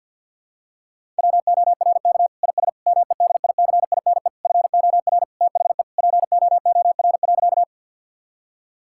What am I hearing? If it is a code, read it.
Morse code, heard as a keyed tone, about 50 wpm, 700 Hz. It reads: WORK IS MEDICINE FOR THE POOR.
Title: Sentences from Top 1000 Words - Review Part 6 - 50wpm